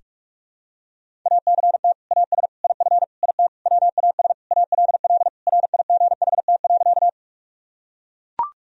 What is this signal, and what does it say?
Morse code, heard as a keyed tone, about 41 wpm, 700 Hz. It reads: ACT AS IF IT WAS ALL RIGHT.
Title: Sentences from Top 1000 Words - Review Part 1 - 45wpm